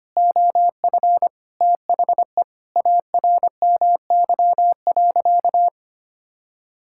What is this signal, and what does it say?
Morse code, heard as a keyed tone, about 25 wpm, 700 Hz. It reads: OF THE ARMY.
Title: Sentences from Top 1600 Words 25wpm